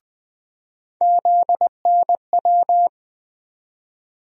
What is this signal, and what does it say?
Morse code, heard as a keyed tone, about 20 wpm, 700 Hz. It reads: ZNW
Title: USA Call Sign Postfixes - 3 Letters 20wpm